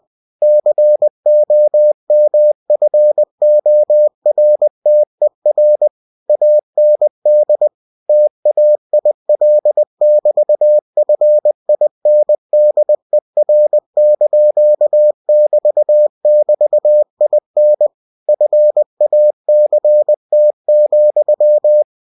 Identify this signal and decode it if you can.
Morse code, heard as a keyed tone, about 20 wpm, 600 Hz. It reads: COMFORTER AND TAIL<BT>FINDER)<BT><BT>IN FACT,
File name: cw-books / winnie-the-pooh / winnie-20wpm-page-0099